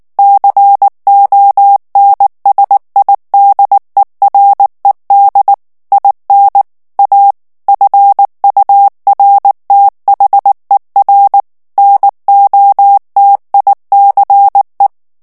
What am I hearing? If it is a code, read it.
Morse code, heard as a keyed tone, about 19 wpm, 800 Hz. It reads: CONSIDERED IN A FURTHER NOTICE
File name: ANGLAIS texte clair 3 vitesse 18 mots minute